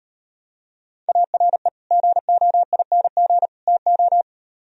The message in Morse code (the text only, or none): ARE GOING TO